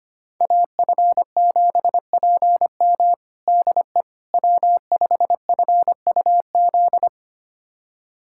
Morse code, heard as a keyed tone, about 25 wpm, 700 Hz. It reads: AF7PM DE W5FUZ
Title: USA Callsign DE Callsign - 5 Characters - 25wpm